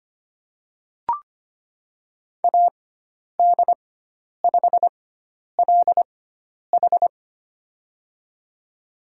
Morse code, heard as a keyed tone, about 25 wpm, spaced as for 10 wpm, 700 Hz. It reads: AD5LH
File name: Lesson 16g - Letter c - Call Signs 25wpm